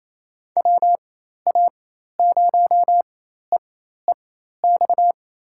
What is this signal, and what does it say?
Morse code, heard as a keyed tone, about 27 wpm, spaced as for 13 wpm, 700 Hz. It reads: WA0EEX
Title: USA Call Signs - 6 Characters - 4x Character Spacing - 28wpm